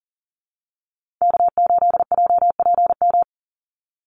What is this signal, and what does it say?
Morse code, heard as a keyed tone, about 40 wpm, 700 Hz. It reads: K8JPM
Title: Call Signs - 40wpm